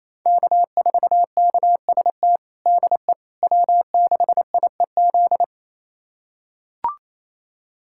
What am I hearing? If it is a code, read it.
Morse code, heard as a keyed tone, about 28 wpm, 700 Hz. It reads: K4KST DE W6IEZ
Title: USA Callsign DE Callsign - 5 Characters - 28wpm